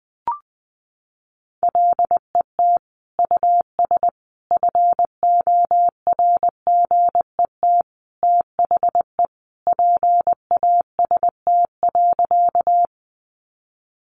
Morse code, heard as a keyed tone, about 20 wpm, 700 Hz. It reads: LET US FORGET THE PAST.
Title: Sentences from Top 1200 Words 20wpm